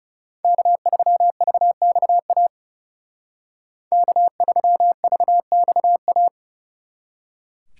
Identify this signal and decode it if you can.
Morse code, heard as a keyed tone, about 32 wpm, 700 Hz. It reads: K3VXA K3VXA
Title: Speed Racing - USA 1x3 Call Signs 30wpm